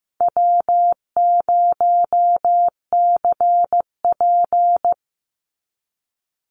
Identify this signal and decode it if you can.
Morse code, heard as a keyed tone, about 15 wpm, 700 Hz. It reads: W0CP
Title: NA SOTA Activator Callsigns 15wpm